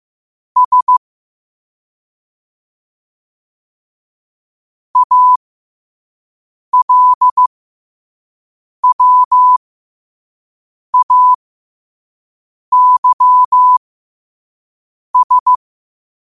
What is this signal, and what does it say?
Morse code, heard as a keyed tone, about 15 wpm, spaced as for 5 wpm, 1000 Hz. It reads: S ALWAYS
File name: qotdfiles / 5wpm